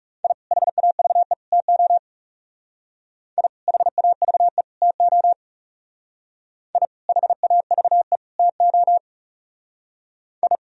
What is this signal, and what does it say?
Morse code, heard as a keyed tone, about 37 wpm, 700 Hz. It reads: I HAVE TO I HAVE TO I HAVE TO I
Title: Speed-Racing - Sets of 3 Words 30wpm